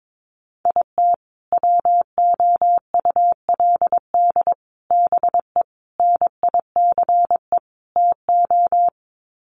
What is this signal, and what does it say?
Morse code, heard as a keyed tone, about 22 wpm, 700 Hz. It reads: IT WOULD BE NICE TO
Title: Sets of 5 Words Encore2 22wpm